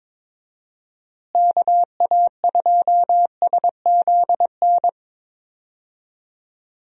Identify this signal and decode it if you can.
Morse code, heard as a keyed tone, about 22 wpm, 700 Hz. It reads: KA2SZN
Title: USA Call Signs - 6 Characters - Repeated 3x - 22wpm